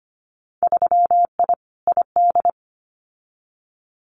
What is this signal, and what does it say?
Morse code, heard as a keyed tone, about 25 wpm, 700 Hz. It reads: 3I ID
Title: Winter Field Day Exchanges Encore 25wpm